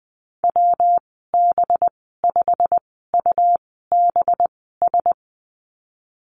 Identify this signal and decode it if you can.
Morse code, heard as a keyed tone, about 19 wpm, spaced as for 14 wpm, 700 Hz. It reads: WB5UBS
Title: 207 Lesson 33g - Letter j - Call Signs 20wpm